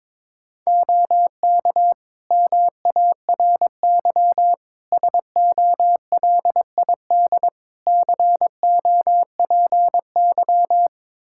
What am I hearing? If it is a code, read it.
Morse code, heard as a keyed tone, about 22 wpm, 700 Hz. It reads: OK MARY SOLID COPY
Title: Practice QSO - Exchange 3 - More Info - 22wpm